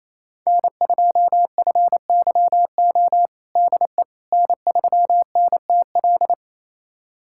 Morse code, heard as a keyed tone, about 28 wpm, 700 Hz. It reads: N2FYO DE N3NTL
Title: USA Callsign DE Callsign - 5 Characters - 28wpm